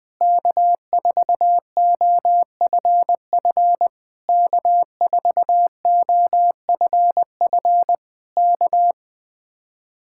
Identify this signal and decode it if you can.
Morse code, heard as a keyed tone, about 21 wpm, 700 Hz. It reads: K4OFF K4OFF K
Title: Speed-Racing Common QSO Phrases - 20wpm